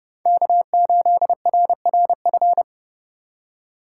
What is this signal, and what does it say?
Morse code, heard as a keyed tone, about 32 wpm, 700 Hz. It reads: K8RRF
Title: USA Call Signs - 5 Characters - 3x Character Spacing - 30wpm